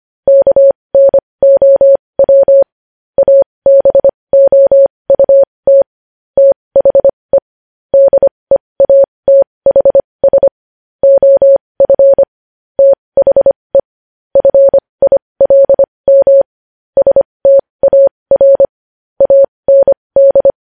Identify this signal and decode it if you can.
Morse code, heard as a keyed tone, about 25 wpm, spaced as for 20 wpm, 550 Hz. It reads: KNOW ABOUT THE DEATHS OF THE FILM STAR AND
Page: News Headlines in Morse Code at 20 WPM